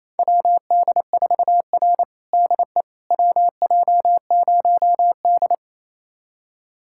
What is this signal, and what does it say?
Morse code, heard as a keyed tone, about 27 wpm, 700 Hz. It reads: WD4R DE WJ0D
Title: USA Callsign DE Callsign - 4 Characters - 28wpm